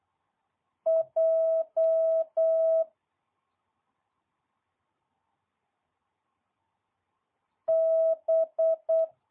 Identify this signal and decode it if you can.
Morse code, heard as a keyed tone, about 8 wpm, 650 Hz. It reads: J B
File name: jh8cbh / 2022cw